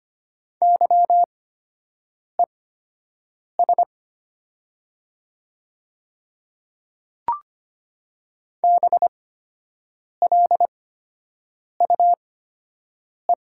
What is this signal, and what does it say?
Morse code, heard as a keyed tone, about 25 wpm, spaced as for 7 wpm, 700 Hz. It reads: YES BLUE
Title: 8x Character Spacing - Top 500 Words - 25wpm